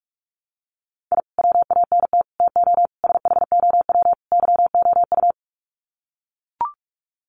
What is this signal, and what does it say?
Morse code, heard as a keyed tone, about 44 wpm, 700 Hz. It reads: I WANT TO SHOW YOU
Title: Sets of 5 Words 45wpm